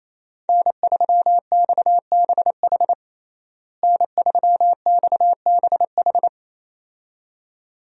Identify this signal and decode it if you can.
Morse code, heard as a keyed tone, about 27 wpm, 700 Hz. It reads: N3XBH N3XBH
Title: USA Call Signs - 5 Characters - Repeated 2x - 28wpm